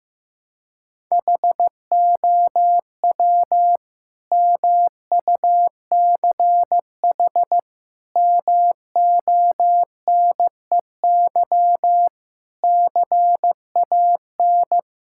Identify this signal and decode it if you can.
Morse code, heard as a keyed tone, about 15 wpm, 700 Hz. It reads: HOW MUCH MONEY CAN
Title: Sentences from Top 1000 Words - Review Part 5 - 15wpm